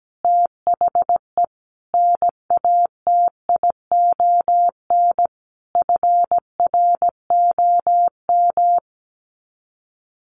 Morse code, heard as a keyed tone, about 17 wpm, 700 Hz. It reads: THE NATION FROM
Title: Sets of 5 Words 17wpm